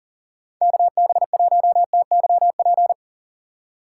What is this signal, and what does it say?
Morse code, heard as a keyed tone, about 50 wpm, 700 Hz. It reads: KB1TYP
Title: Speed Racing - USA 2x3 Call Signs 40wpm